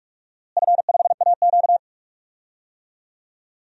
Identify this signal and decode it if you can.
Morse code, heard as a keyed tone, about 42 wpm, 700 Hz. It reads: F5AQ
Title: DXCC-100 Call Signs - 4 Characters - Repeated 2x - 45wpm